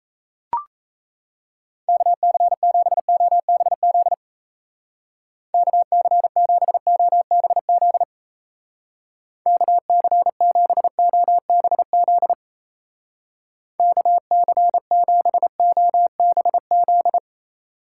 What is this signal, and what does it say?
Morse code, heard as a keyed tone, about 34 wpm, 700 Hz. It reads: KC7OBZ KC7OBZ KC7OBZ KC7OBZ
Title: Speed Racing - USA 2x3 Call Signs 28wpm